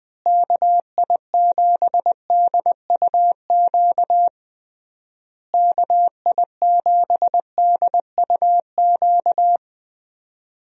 Morse code, heard as a keyed tone, about 20 wpm, 700 Hz. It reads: KI7DUQ KI7DUQ
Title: USA Call Signs - 6 Characters - Repeated 3x - 20wpm